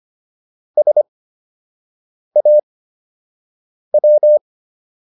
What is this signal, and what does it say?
Morse code, heard as a keyed tone, about 25 wpm, spaced as for 6 wpm, 600 Hz. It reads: SAW